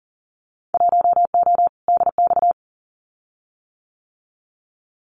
Morse code, heard as a keyed tone, about 40 wpm, 700 Hz. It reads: 1O DX
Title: Winter Field Day Exchanges 40wpm